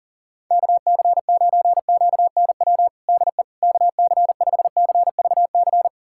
Morse code, heard as a keyed tone, about 40 wpm, 700 Hz. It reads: KC9QNW DE KC5CVC
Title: USA Callsign DE Callsign 40wpm